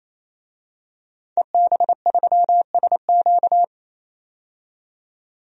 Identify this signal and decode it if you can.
Morse code, heard as a keyed tone, about 28 wpm, 700 Hz. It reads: EB3SQ
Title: DXCC-100 Call Signs - 5 Characters - 28wpm